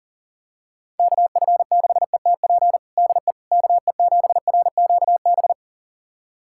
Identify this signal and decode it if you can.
Morse code, heard as a keyed tone, about 40 wpm, 700 Hz. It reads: KF6ETP DE KE7RQB
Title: USA Callsign DE Callsign - 6 Characters - 40wpm